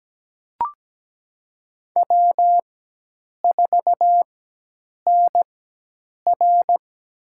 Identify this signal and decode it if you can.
Morse code, heard as a keyed tone, about 17 wpm, spaced as for 8 wpm, 700 Hz. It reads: W4NR
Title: USA Call Signs - 4 Characters - 4x Character Spacing - 17wpm